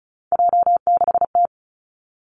Morse code, heard as a keyed tone, about 34 wpm, 700 Hz. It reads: J6T
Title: Three Challenging Letter-Number Combinations 35wpm